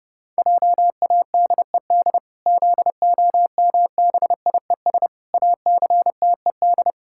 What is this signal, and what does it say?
Morse code, heard as a keyed tone, about 30 wpm, 700 Hz. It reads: JADED ZOMBIES ACTED
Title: Pangram Challenge v2 - 30wpm